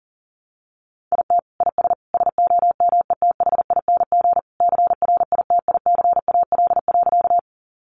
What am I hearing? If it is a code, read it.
Morse code, heard as a keyed tone, about 37 wpm, 700 Hz. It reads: IT IS SOMETHING CRITICAL.
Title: Sentences from Top 1100 Words 40wpm